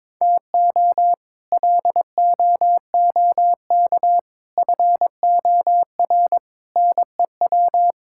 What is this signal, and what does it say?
Morse code, heard as a keyed tone, about 21 wpm, 700 Hz. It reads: TO LOOK FOR NEW
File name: practice / 22wpm / Sets of 6 Words - 22wpm